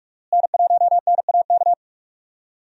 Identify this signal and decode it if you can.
Morse code, heard as a keyed tone, about 41 wpm, 700 Hz. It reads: N1NAK
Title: USA Call Signs - 5 Characters - Repeated 2x - 45wpm